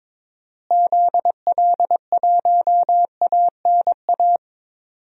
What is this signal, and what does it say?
Morse code, heard as a keyed tone, about 22 wpm, 700 Hz. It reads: ZL1ANA
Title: DXCC-100 Call Signs - 6 Characters - Repeated 3x - 22wpm